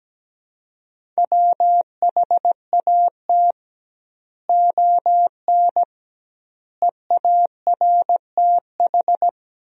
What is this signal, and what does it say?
Morse code, heard as a keyed tone, about 17 wpm, 700 Hz. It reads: WHAT ON EARTH
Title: 2x Word-Spacing - Sentences from Top 200 Words - 17wpm